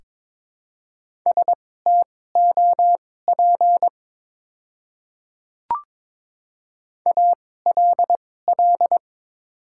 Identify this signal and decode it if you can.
Morse code, heard as a keyed tone, about 22 wpm, spaced as for 16 wpm, 700 Hz. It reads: STOP ALL
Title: 2x Character Spacing - Top 300 Words - 22wpm